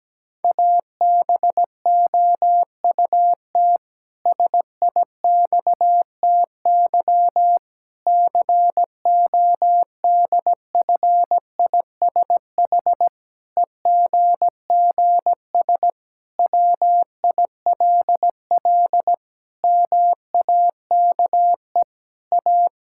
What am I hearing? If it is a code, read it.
Morse code, heard as a keyed tone, about 17 wpm, 700 Hz. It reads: ABOUT SIXTY CODFISH EGGS WILL MAKE A
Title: Pangram Challenge v2 - 17wpm